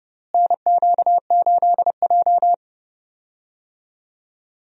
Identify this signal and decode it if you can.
Morse code, heard as a keyed tone, about 31 wpm, 700 Hz. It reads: NQ8J
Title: USA Call Signs - 4 Characters - Repeated 2x - 30wpm